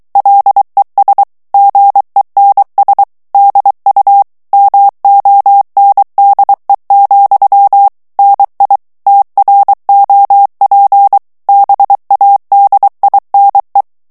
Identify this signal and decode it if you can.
Morse code, heard as a keyed tone, about 23 wpm, 800 Hz. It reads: LES GENS DU MONDE, NI TROP BADINE
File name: FRANCAIS TEXTE CLAIR 2 VITESSE 22 MOTS MINUTES